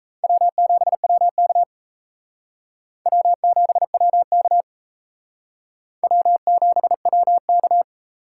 Morse code, heard as a keyed tone, about 34 wpm, 700 Hz. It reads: W7WK W7WK W7WK
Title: Speed Racing - USA 1x2 Call Signs 28wpm